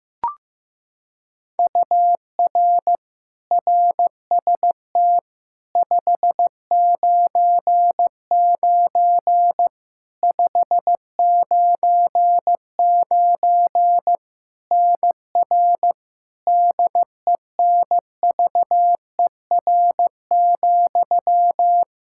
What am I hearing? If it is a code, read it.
Morse code, heard as a keyed tone, about 15 wpm, 700 Hz. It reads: UR RST 599 599 NR DENVER,
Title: Practice QSO - Exchange 1 - Beginning - 15wpm